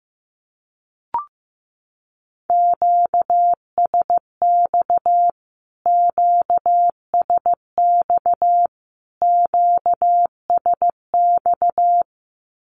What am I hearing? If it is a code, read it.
Morse code, heard as a keyed tone, about 15 wpm, 700 Hz. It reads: QSX QSX QSX
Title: Q-Codes Repeated 3x 15wpm